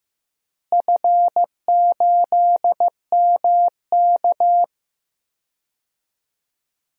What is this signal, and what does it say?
Morse code, heard as a keyed tone, about 15 wpm, 700 Hz. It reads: F8MK
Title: DXCC-100 Call Signs - 4 Characters - Repeated 2x - 15wpm